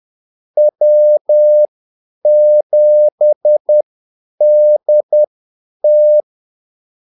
Morse code, heard as a keyed tone, about 10 wpm, spaced as for 8 wpm, 600 Hz. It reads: W7DT
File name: Callsigns-10-8wpm